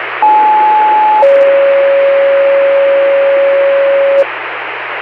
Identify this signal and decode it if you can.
Two-tone page, heard as a one-second tone, then a three-second tone, about 860 then 550 Hz, with noise. It is heard from 0.2 to 4.2 s.